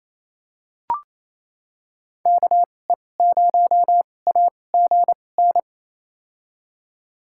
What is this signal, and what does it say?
Morse code, heard as a keyed tone, about 28 wpm, spaced as for 20 wpm, 700 Hz. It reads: KE0AGN
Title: USA Call Signs - 6 Characters - 2x Character Spacing - 28wpm